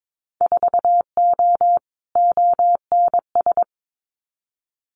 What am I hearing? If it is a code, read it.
Morse code, heard as a keyed tone, about 22 wpm, 700 Hz. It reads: 4O ONS
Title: Winter Field Day Exchanges 22wpm